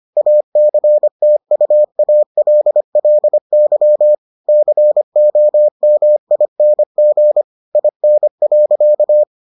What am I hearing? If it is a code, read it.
Morse code, heard as a keyed tone, about 25 wpm, 600 Hz. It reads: ACTUALLY COMING IN.